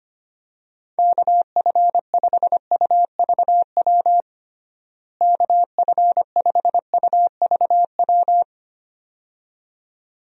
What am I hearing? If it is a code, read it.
Morse code, heard as a keyed tone, about 25 wpm, 700 Hz. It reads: KF5UVW KF5UVW
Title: USA Call Signs - 6 Characters - Repeated 2x - 25wpm